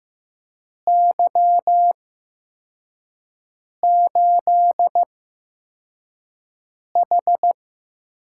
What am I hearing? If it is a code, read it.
Morse code, heard as a keyed tone, about 15 wpm, spaced as for 4 wpm, 700 Hz. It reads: Y8H
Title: USA Call Signs - 4 Characters - 8x Character Spacing - 15wpm